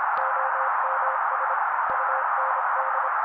Morse code, heard as a keyed tone, about 26 wpm, 530 Hz. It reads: OMS UND
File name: daily / 25wpm noise